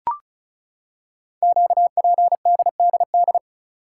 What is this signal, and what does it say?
Morse code, heard as a keyed tone, about 36 wpm, 700 Hz. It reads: QPDDD
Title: Code Groups - 5 Letters - 35wpm